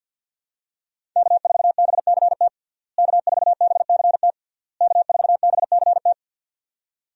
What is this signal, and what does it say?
Morse code, heard as a keyed tone, about 50 wpm, 700 Hz. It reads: K4BCT K4BCT K4BCT
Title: USA Call Signs - 5 Characters - Repeated 3x - 50wpm